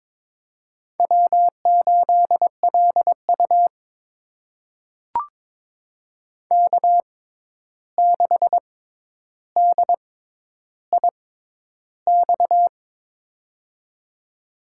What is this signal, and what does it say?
Morse code, heard as a keyed tone, about 22 wpm, spaced as for 11 wpm, 700 Hz. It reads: W8LU K6DIX
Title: USA Call Signs - 6x Character Spacing - 22wpm